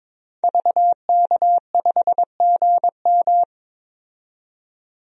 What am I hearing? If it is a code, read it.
Morse code, heard as a keyed tone, about 22 wpm, 700 Hz. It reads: VK5GM
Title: DXCC-100 Call Signs - 5 Characters - 22wpm